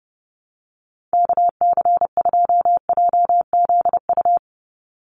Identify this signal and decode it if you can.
Morse code, heard as a keyed tone, about 30 wpm, 700 Hz. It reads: KC2JZU
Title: USA Call Signs 30wpm